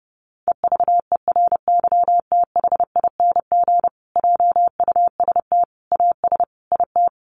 Morse code, heard as a keyed tone, about 30 wpm, 700 Hz. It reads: EVERYTHING JUST AS IT
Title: Sentences from Top 700 Words 30wpm